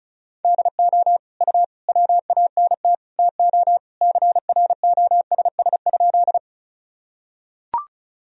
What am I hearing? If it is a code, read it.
Morse code, heard as a keyed tone, about 34 wpm, 700 Hz. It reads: DO U WANT TO CROSS?
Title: Sentences from Top 300 Words and Abbreviations 35wpm